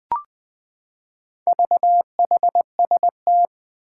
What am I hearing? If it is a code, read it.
Morse code, heard as a keyed tone, about 20 wpm, 700 Hz. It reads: VHST
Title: Four Challenging Letter-Number Combinations 20wpm